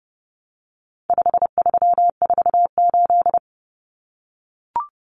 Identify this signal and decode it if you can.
Morse code, heard as a keyed tone, about 28 wpm, 700 Hz. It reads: 5348
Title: Common Cut Numbers - 4-Digit 30wpm